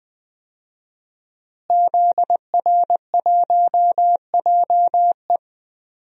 Morse code, heard as a keyed tone, about 20 wpm, 700 Hz. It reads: ZR1JE
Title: DXCC-100 Call Signs - 5 Characters - Repeated 3x - 20wpm